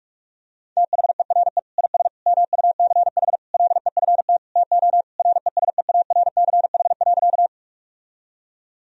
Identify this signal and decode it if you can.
Morse code, heard as a keyed tone, about 47 wpm, 700 Hz. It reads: THERE IS MUCH LEFT TO RESEARCH.
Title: Sentences from Top 1000 Words - Review Part 4 - 45wpm